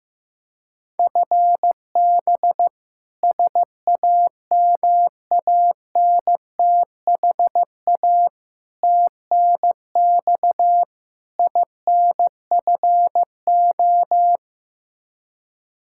Morse code, heard as a keyed tone, about 15 wpm, 700 Hz. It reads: FB SAMANTHA TNX INFO
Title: K1USN SST QSO Practice 15wpm